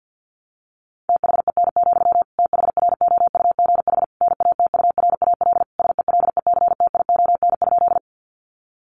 Morse code, heard as a keyed tone, about 50 wpm, 700 Hz. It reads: THEN, THROUGH NATURAL SELECTION?
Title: Sentences from Top 1400 Words 50wpm